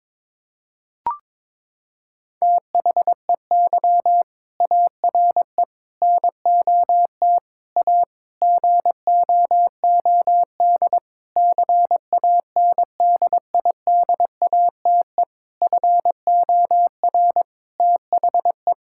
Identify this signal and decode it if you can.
Morse code, heard as a keyed tone, about 22 wpm, 700 Hz. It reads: THEY ARE NOT A GOOD CANDIDATE FOR THE
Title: Sentences from Top 1600 Words 22wpm